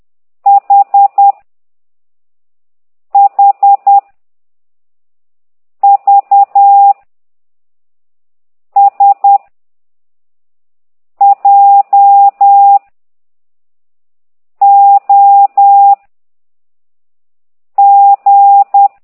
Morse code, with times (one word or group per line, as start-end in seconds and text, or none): 0.5-1.3 H
3.1-4.0 H
5.8-6.9 V
8.8-9.4 S
11.2-12.8 J
14.6-15.9 O
17.8-18.9 G